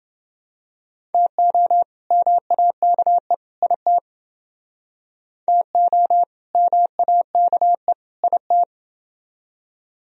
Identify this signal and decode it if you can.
Morse code, heard as a keyed tone, about 30 wpm, 700 Hz. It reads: TO MAKE IT TO MAKE IT